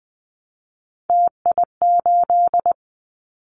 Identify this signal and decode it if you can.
Morse code, heard as a keyed tone, about 20 wpm, 700 Hz. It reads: TI8
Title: Callsign Prefixes for Top 100 DXCC - 20wpm